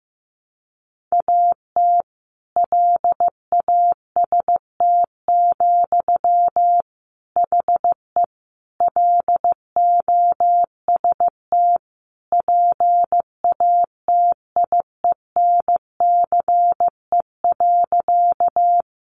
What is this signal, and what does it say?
Morse code, heard as a keyed tone, about 15 wpm, 700 Hz. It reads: AT LAST, HE LOST PATIENCE.
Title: Sentences from Top 1400 Words 15wpm